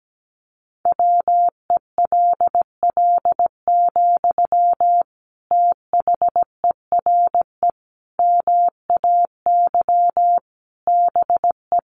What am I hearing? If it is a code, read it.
Morse code, heard as a keyed tone, about 17 wpm, 700 Hz. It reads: WELL, THERE MAY BE
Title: Sentences from Top 1300 Words 17wpm